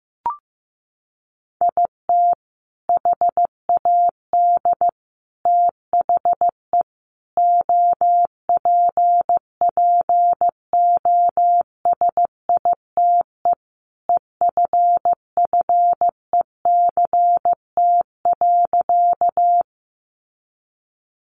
Morse code, heard as a keyed tone, about 15 wpm, 700 Hz. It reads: IT HAD THE OPPOSITE EFFECT.
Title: Sentences from Top 1300 Words 15wpm